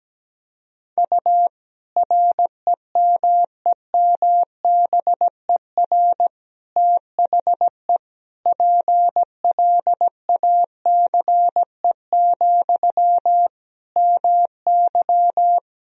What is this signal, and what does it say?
Morse code, heard as a keyed tone, about 17 wpm, 700 Hz. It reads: U REMEMBER THE PLACE, MY
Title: Sentences from Top 400 Words and Abbreviations 17wpm